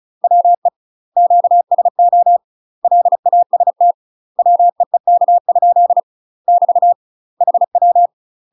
Morse code, heard as a keyed tone, about 36 wpm, 700 Hz. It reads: WE QSO LAST WEEK? <BT> HW
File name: UK MASTER QSOs TRAINING / 35WPM / UK MASTER QSOs 0009